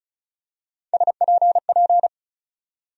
Morse code, heard as a keyed tone, about 36 wpm, 700 Hz. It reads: SPP
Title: USA Call Sign Postfixes - 3 Letters 35wpm